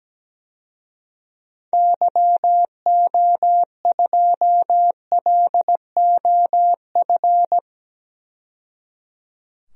Morse code, heard as a keyed tone, about 17 wpm, 700 Hz. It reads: YO2LOF
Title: DXCC-100 Call Signs - 6 Characters - 17wpm